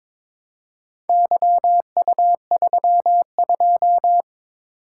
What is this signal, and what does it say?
Morse code, heard as a keyed tone, about 22 wpm, 700 Hz. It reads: YU32
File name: practice / 22wpm / Four Letter-Number Combinations - 22wpm